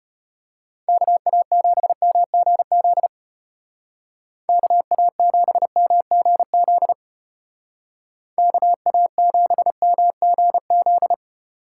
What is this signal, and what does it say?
Morse code, heard as a keyed tone, about 33 wpm, 700 Hz. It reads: KA7MGZ KA7MGZ KA7MGZ